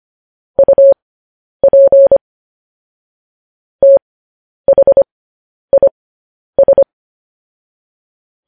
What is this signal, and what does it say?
Morse code, heard as a keyed tone, about 25 wpm, spaced as for 10 wpm, 550 Hz. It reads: UP THIS
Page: news headlines in morse code at 10 wpm